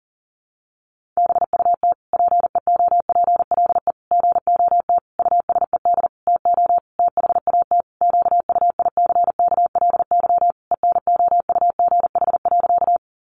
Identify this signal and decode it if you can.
Morse code, heard as a keyed tone, about 40 wpm, 700 Hz. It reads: BUT PEOPLE GOT USED TO THAT QUICKLY ENOUGH.
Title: Sentences from Top 700 Words 40wpm